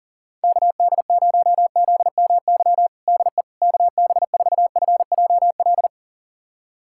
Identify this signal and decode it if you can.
Morse code, heard as a keyed tone, about 40 wpm, 700 Hz. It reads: KD0ZMY DE KB4FJL
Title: USA Callsign DE Callsign - 6 Characters - 40wpm